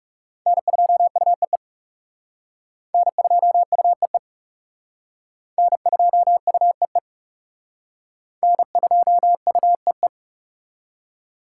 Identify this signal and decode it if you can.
Morse code, heard as a keyed tone, about 40 wpm, 700 Hz. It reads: N2UEE N2UEE N2UEE N2UEE